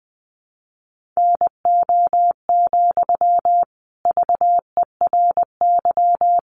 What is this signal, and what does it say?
Morse code, heard as a keyed tone, about 20 wpm, 700 Hz. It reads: NO, VERY
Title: Sentences from Top 1500 Words 20wpm